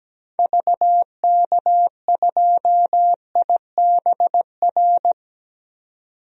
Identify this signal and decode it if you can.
Morse code, heard as a keyed tone, about 17 wpm, 700 Hz. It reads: VK2IBR
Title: DXCC-100 Call Signs - 6 Characters - 17wpm